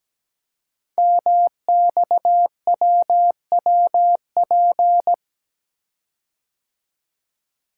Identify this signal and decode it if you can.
Morse code, heard as a keyed tone, about 17 wpm, 700 Hz. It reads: MXWWP
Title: Code Groups - 5 Letters - 17wpm